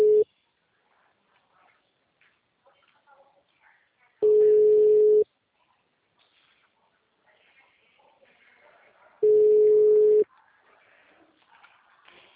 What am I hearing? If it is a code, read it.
Ringback tone (Europe, 425 Hz), with speech.
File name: out- -2019- - - .16070